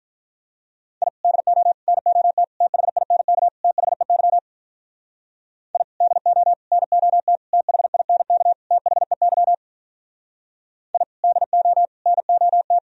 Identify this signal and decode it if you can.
Morse code, heard as a keyed tone, about 46 wpm, 700 Hz. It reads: I DO NOT THINK THEY I DO NOT THINK THEY I DO NOT